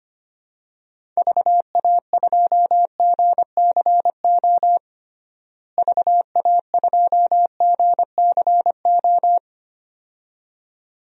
Morse code, heard as a keyed tone, about 25 wpm, 700 Hz. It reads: VA2GCO VA2GCO
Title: DXCC-100 Call Signs - 6 Characters - Repeated 2x - 25wpm